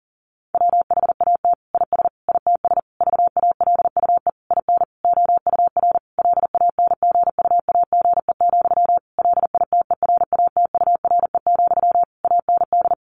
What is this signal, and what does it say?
Morse code, heard as a keyed tone, about 40 wpm, 700 Hz. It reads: WHAT IS ITS VALUE IN OUR LANGUAGE, LITERATURE, AND
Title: Sentences from Top 600 Words 40wpm